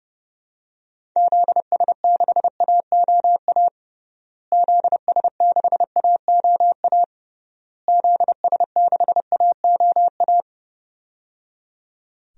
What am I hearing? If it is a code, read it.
Morse code, heard as a keyed tone, about 30 wpm, 700 Hz. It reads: ZS6AOA ZS6AOA ZS6AOA
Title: DXCC-100 Call Signs - 6 Characters - Repeated 3x - 30wpm